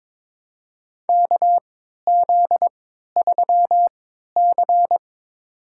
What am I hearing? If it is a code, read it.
Morse code, heard as a keyed tone, about 22 wpm, spaced as for 12 wpm, 700 Hz. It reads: KZ3C
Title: USA Call Signs - 4 Characters - 3x Character Spacing - 22wpm